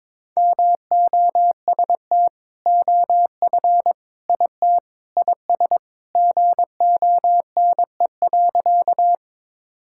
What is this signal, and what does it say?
Morse code, heard as a keyed tone, about 22 wpm, 700 Hz. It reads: MOST OF IT IS GONE.
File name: practice / 22wpm / Sentences with 2-Word Phrases - 22wpm